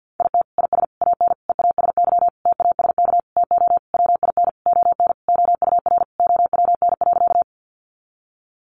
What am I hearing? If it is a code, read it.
Morse code, heard as a keyed tone, about 50 wpm, 700 Hz. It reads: IT IS AN EASY TASK TO WIN ON OUR OWN.
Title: Sentences from Top 1500 Words 50wpm